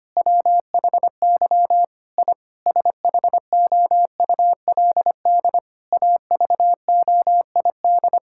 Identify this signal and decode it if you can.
Morse code, heard as a keyed tone, about 25 wpm, 700 Hz. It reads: WHY I SHOULD AVOID
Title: Sentences from Top 1000 Words - Review Part 6 - 25wpm